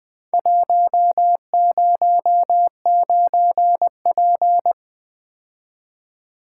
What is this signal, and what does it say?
Morse code, heard as a keyed tone, about 21 wpm, 700 Hz. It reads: 109P
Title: Four Letter-Number Combinations - 20wpm